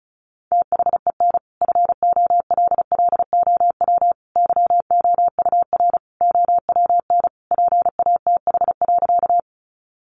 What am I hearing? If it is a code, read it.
Morse code, heard as a keyed tone, about 34 wpm, 700 Hz. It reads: THEN FOLLOW YOUR OWN PATH.
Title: Sentences from Top 1400 Words 35wpm